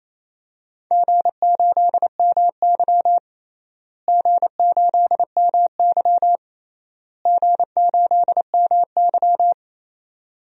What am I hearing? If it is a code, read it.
Morse code, heard as a keyed tone, about 28 wpm, 700 Hz. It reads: G8MY G8MY G8MY
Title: DXCC-100 Call Signs - 4 Characters - Repeated 3x - 28wpm